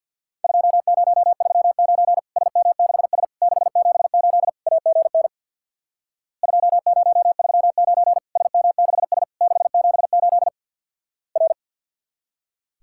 Morse code, heard as a keyed tone, about 48 wpm, 700 Hz. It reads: 2039 SM6S 678 AGN 2039 SM6S 678 R
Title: Worked All Europe DX Contest - QTC Exchange with Repeats - 50wpm